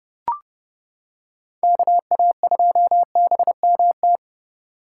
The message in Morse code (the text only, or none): KA2BMT